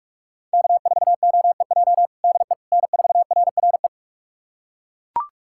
Morse code, heard as a keyed tone, about 42 wpm, 700 Hz. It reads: K4OEJ DE N4RRE
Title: USA Callsign DE Callsign - 5 Characters - 45wpm